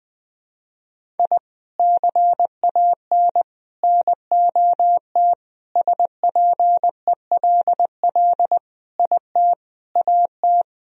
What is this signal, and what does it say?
Morse code, heard as a keyed tone, about 21 wpm, 700 Hz. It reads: I CAN NOT SPELL IT AT
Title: Sentences from Top 1000 Words - Review Part 1 - 20wpm